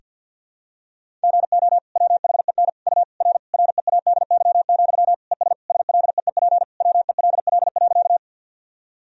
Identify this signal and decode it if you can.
Morse code, heard as a keyed tone, about 49 wpm, 700 Hz. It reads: GO WHEN U R READY, ES SLEEP WELL.
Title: Sentences from Top 500 Words and Abbreviations 50wpm